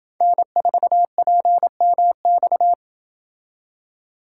Morse code, heard as a keyed tone, about 28 wpm, 700 Hz. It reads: N4PMX